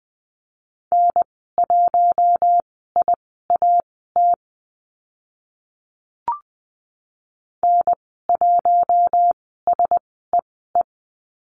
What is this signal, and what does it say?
Morse code, heard as a keyed tone, about 20 wpm, spaced as for 15 wpm, 700 Hz. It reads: N1IAT N1SEE
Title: 042 Lesson 8g - Number 1 - Call Signs 20wpm